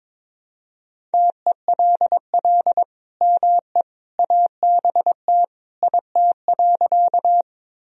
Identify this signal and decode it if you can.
Morse code, heard as a keyed tone, about 22 wpm, 700 Hz. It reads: TELL ME ABT IT.
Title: Sentences from Top 200 Words and Abbreviations 22wpm